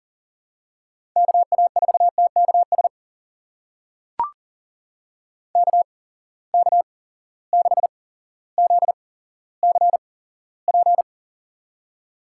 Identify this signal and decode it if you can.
Morse code, heard as a keyed tone, about 40 wpm, spaced as for 17 wpm, 700 Hz. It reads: KA4TKS KK6ZCP
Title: USA Call Signs - 8x Character Spacing - 40wpm